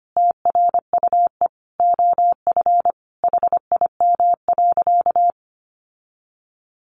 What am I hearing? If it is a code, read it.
Morse code, heard as a keyed tone, about 25 wpm, 700 Hz. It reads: TRUE OF HIM.
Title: Sentences from Top 700 Words 25wpm